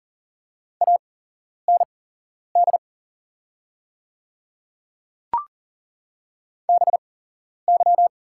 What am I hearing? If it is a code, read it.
Morse code, heard as a keyed tone, about 39 wpm, spaced as for 11 wpm, 700 Hz. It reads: AND BY
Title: 8x Character Spacing - Top 200 Words - 40wpm